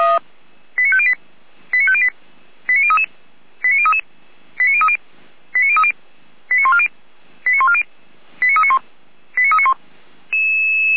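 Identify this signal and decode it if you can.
Five-tone selective call (ZVEI), heard as quick runs of five tones, tones of about 70 ms. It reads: 89508 89508 89033 89033 89030 89030 89140 89140 89391 89391
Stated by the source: ZVEI five-tone selective call received by SDR.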